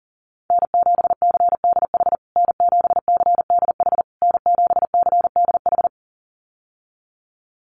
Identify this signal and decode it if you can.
Morse code, heard as a keyed tone, about 40 wpm, 700 Hz. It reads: N7CDH N7CDH N7CDH
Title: USA Call Signs Repeated 3x 40wpm